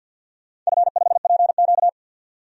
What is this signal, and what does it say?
Morse code, heard as a keyed tone, about 51 wpm, 700 Hz. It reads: F5PQ
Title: DXCC-100 Call Signs - 4 Characters - 50wpm